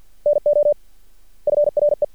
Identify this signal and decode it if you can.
Morse code, heard as a keyed tone, about 53 wpm, 590 Hz. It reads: NO FRE
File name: archived / CRSnet